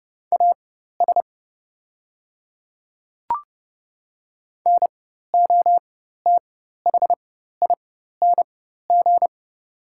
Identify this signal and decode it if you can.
Morse code, heard as a keyed tone, about 30 wpm, spaced as for 14 wpm, 700 Hz. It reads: AS NOTHING
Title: 4x Character Spacing - Top 500 Words - 30wpm